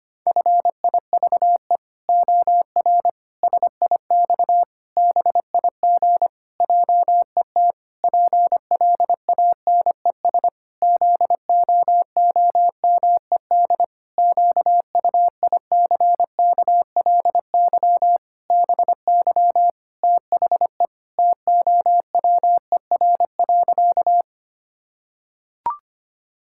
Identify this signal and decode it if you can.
Morse code, heard as a keyed tone, about 25 wpm, 700 Hz. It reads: FIVE OR SIX BIG JET PLANES ZOOMED QUICKLY BY THE TOWER.
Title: Pangram Challenge v2 - 25wpm